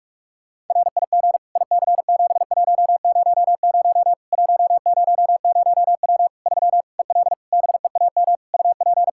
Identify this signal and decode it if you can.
Morse code, heard as a keyed tone, about 46 wpm, 700 Hz. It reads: RIG IC7100 100W 3 EL BEAM UP
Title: Practice QSO - Exchange 2 - Info - 45wpm